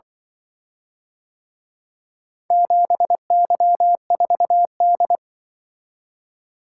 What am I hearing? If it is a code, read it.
Morse code, heard as a keyed tone, about 23 wpm, 700 Hz. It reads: 7Y4D
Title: Speed-Racing - Four Letter-Number Combinations - 20wpm